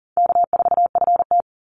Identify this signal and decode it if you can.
Morse code, heard as a keyed tone, about 40 wpm, 700 Hz. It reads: K4FT
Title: Call Signs - 40wpm